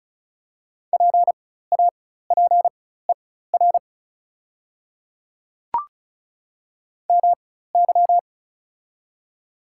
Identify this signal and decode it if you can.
Morse code, heard as a keyed tone, about 35 wpm, spaced as for 16 wpm, 700 Hz. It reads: PAPER MY
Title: 4x Character Spacing - Top 300 Words - 35wpm